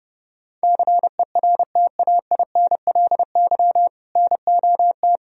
Morse code, heard as a keyed tone, about 30 wpm, 700 Hz. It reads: CERTAINLY NOT
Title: Sentences from Top 1000 Words - Review Part 1 - 30wpm